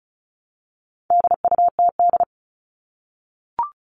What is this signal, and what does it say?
Morse code, heard as a keyed tone, about 34 wpm, 700 Hz. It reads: DUTD
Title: All Cut Numbers - 4-Digit 35wpm